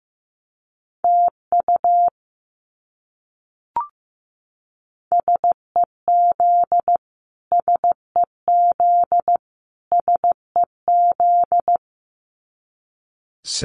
Morse code, heard as a keyed tone, about 15 wpm, 700 Hz. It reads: TU SEZ SEZ SEZ
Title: Abbreviations Repeated 3x - 15wpm